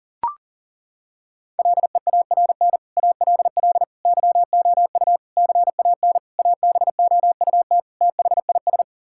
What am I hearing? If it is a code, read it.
Morse code, heard as a keyed tone, about 40 wpm, 700 Hz. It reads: LEARN ALL YOU CAN ABOUT THIS
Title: Sentences from Top 1000 Words - Review Part 5 - 40wpm